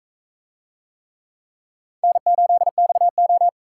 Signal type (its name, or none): Morse code